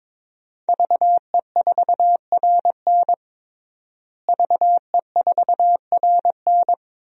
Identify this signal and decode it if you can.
Morse code, heard as a keyed tone, about 22 wpm, 700 Hz. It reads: VE4RN VE4RN
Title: DXCC-100 Call Signs - 5 Characters - Repeated 3x - 22wpm